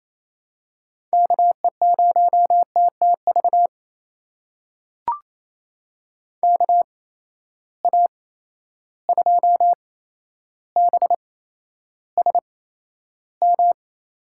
Morse code, heard as a keyed tone, about 28 wpm, spaced as for 12 wpm, 700 Hz. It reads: KE0TTV KA2BSM
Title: USA Call Signs - 6 Characters - 8x Character Spacing - 28wpm